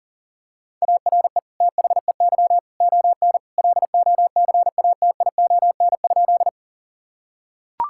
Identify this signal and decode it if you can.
Morse code, heard as a keyed tone, about 40 wpm, 700 Hz. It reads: ARE THEY ON LOCATION?
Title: Sentences from Top 1000 Words - Review Part 6 - 40wpm